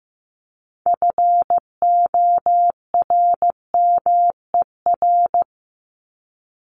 Morse code, heard as a keyed tone, about 15 wpm, 700 Hz. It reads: FORMER
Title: LY Suffix - Top 100 Words with Base Words 15wpm